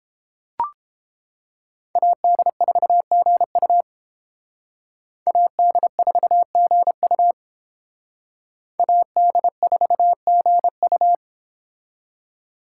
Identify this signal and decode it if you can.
Morse code, heard as a keyed tone, about 29 wpm, 700 Hz. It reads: AD4GU AD4GU AD4GU